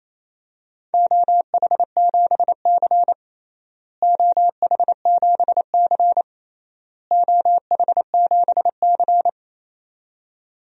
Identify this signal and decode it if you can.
Morse code, heard as a keyed tone, about 28 wpm, 700 Hz. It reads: OH7C OH7C OH7C
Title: DXCC-100 Call Signs - 4 Characters - Repeated 3x - 28wpm